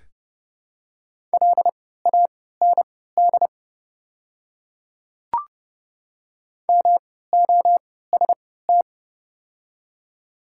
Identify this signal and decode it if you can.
Morse code, heard as a keyed tone, about 32 wpm, spaced as for 17 wpm, 700 Hz. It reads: LAND MOST
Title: 3x Character Spacing - Top 200 Words - 30wpm